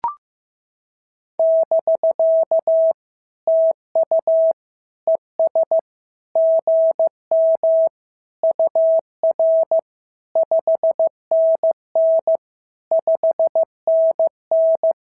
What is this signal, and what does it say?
Morse code, heard as a keyed tone, about 15 wpm, 650 Hz. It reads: <BK> TU ES GM UR 5NN 5NN
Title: SOTA QSO Practice 15wpm